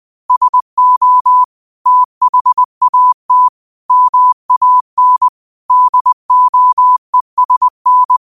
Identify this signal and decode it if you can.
Morse code, heard as a keyed tone, about 20 wpm, 1000 Hz. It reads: SO THAT MAN DOESN
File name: qotdfiles / 20wpm